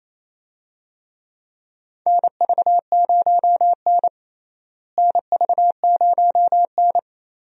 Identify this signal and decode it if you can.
Morse code, heard as a keyed tone, about 28 wpm, 700 Hz. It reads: NV0N NV0N
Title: USA Call Signs - 4 Characters - Repeated 3x - 28wpm